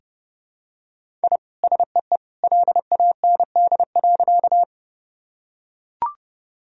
Morse code, heard as a keyed tone, about 30 wpm, 700 Hz. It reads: I SEE LAND.
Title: Sentences from Top 1000 Words - Review Part 1 - 30wpm